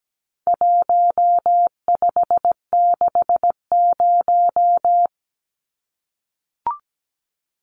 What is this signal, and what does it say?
Morse code, heard as a keyed tone, about 17 wpm, 700 Hz. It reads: 1560
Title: Numbers - Contest Style - 4-Digit 17wpm